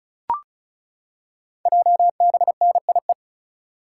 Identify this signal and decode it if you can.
Morse code, heard as a keyed tone, about 36 wpm, 700 Hz. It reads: JBNIE
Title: Code Groups - 5 Letters - 35wpm